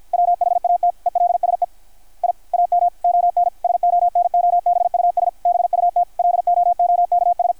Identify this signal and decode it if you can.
Morse code, heard as a keyed tone, about 53 wpm, 660 Hz, with radio noise. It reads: WHAT ELSE I AM ON SONOBUS BUT LOOKS
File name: archived / CRSnet